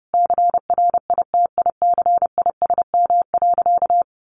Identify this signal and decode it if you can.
Morse code, heard as a keyed tone, about 30 wpm, 700 Hz. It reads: CRITICISM.
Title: Sentences from Top 1300 Words 30wpm